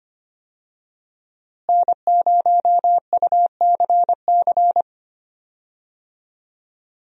Morse code, heard as a keyed tone, about 25 wpm, 700 Hz. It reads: N0UCC
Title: USA Call Signs - 5 Characters - Repeated 2x - 25wpm